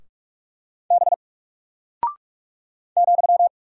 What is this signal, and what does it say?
Morse code, heard as a keyed tone, about 44 wpm, 700 Hz. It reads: B ,
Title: Single Character (Letter:Number:Punctuation:Prosign) 45wpm